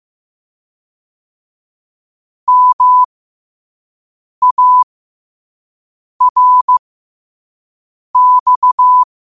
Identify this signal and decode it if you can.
Morse code, heard as a keyed tone, about 15 wpm, spaced as for 5 wpm, 1000 Hz. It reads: MARX
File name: qotdfiles / 5wpm